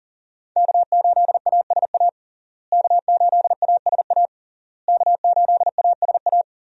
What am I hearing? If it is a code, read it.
Morse code, heard as a keyed tone, about 40 wpm, 700 Hz. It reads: K8ASA K8ASA K8ASA
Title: USA Call Signs - 5 Characters - Repeated 3x - 40wpm